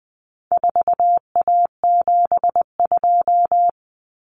Morse code, heard as a keyed tone, about 20 wpm, 700 Hz. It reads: 4A72
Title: Common Cut Numbers - 4-Digit 20wpm